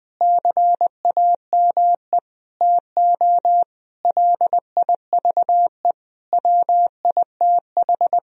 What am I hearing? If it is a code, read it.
Morse code, heard as a keyed tone, about 20 wpm, 700 Hz. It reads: CAME TO LIVE WITH
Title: Sentences from Top 1000 Words - Review Part 3 - 20wpm